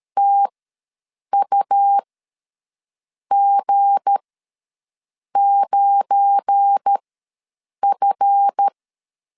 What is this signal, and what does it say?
Morse code, heard as a keyed tone, about 13 wpm, spaced as for 6 wpm, 800 Hz. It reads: TUG9F
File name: LEZIONI / 7 GF9Z / 7 GF9 ws 12wpm 3344